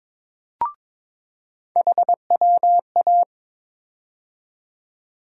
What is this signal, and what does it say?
Morse code, heard as a keyed tone, about 22 wpm, 700 Hz. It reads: HWA
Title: USA Call Sign Postfixes - 3 Letters 22wpm